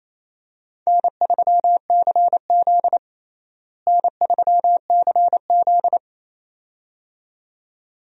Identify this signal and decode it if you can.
Morse code, heard as a keyed tone, about 28 wpm, 700 Hz. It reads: N3CZ N3CZ
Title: USA Call Signs - 4 Characters - Repeated 2x - 28wpm